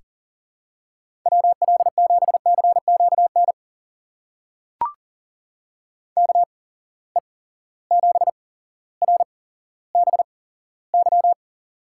Morse code, heard as a keyed tone, about 40 wpm, spaced as for 17 wpm, 700 Hz. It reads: WL7CQN KE7RBY
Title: USA Call Signs - 6 Characters - 8x Character Spacing - 40wpm